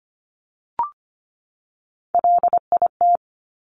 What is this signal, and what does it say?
Morse code, heard as a keyed tone, about 25 wpm, 700 Hz. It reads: LIT
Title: Top Three Letter Combinations in Words 25wpm